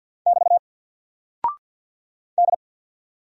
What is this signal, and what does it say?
Morse code, heard as a keyed tone, about 48 wpm, 700 Hz. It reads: <BT> D
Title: Single Character (Letter:Number:Punctuation:Prosign) 50wpm